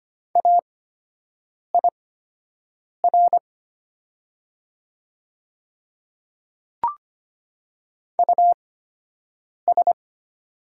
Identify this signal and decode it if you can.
Morse code, heard as a keyed tone, about 25 wpm, spaced as for 7 wpm, 700 Hz. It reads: AIR US
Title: 8x Character Spacing - Top 200 Words - 25wpm